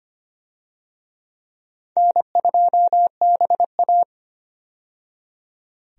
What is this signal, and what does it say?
Morse code, heard as a keyed tone, about 25 wpm, 700 Hz. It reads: N2BA
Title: USA Call Signs - 4 Characters - 25wpm